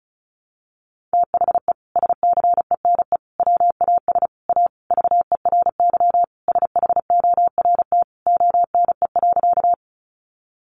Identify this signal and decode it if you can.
Morse code, heard as a keyed tone, about 34 wpm, 700 Hz. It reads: THE SCENE WAS A VERY SHORT ONE.
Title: Sentences from Top 1100 Words 35wpm